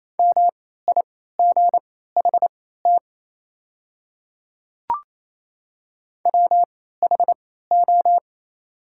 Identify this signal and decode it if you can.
Morse code, heard as a keyed tone, about 28 wpm, spaced as for 16 wpm, 690 Hz. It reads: MIGHT WHO
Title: 3x Character Spacing - Top 300 Words - 28wpm